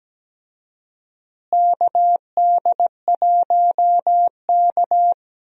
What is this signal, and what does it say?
Morse code, heard as a keyed tone, about 17 wpm, 700 Hz. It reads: KD1K
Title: USA Call Signs - 4 Characters - Repeated 3x - 17wpm